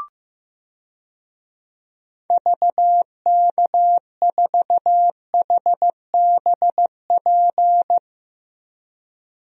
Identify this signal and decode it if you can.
Morse code, heard as a keyed tone, about 15 wpm, 700 Hz. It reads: VK4HBP
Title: DXCC-100 Call Signs - 6 Characters - 15wpm